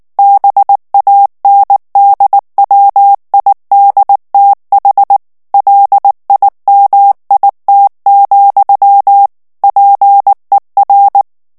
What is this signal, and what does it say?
Morse code, heard as a keyed tone, about 19 wpm, 800 Hz. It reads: BANDWIDTH LIMIT, PER
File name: ANGLAIS texte clair 3 vitesse 18 mots minute